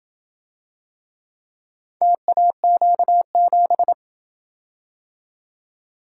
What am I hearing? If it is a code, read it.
Morse code, heard as a keyed tone, about 27 wpm, 700 Hz. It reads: TAQ7